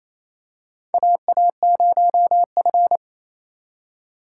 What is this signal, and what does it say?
Morse code, heard as a keyed tone, about 27 wpm, 700 Hz. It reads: AA0F
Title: USA Call Signs - 4 Characters - 4x Character Spacing - 28wpm